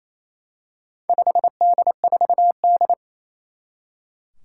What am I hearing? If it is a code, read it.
Morse code, heard as a keyed tone, about 29 wpm, 700 Hz. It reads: 5D4D
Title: Four Challenging Letter-Number Combinations 28wpm